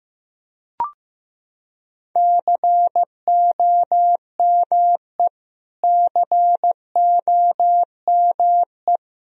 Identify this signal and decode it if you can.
Morse code, heard as a keyed tone, about 16 wpm, 700 Hz. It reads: COME COME
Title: Top 100 Words Repeated 15wpm